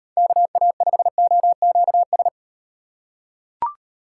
Morse code, heard as a keyed tone, about 38 wpm, 700 Hz. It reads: KA5OQS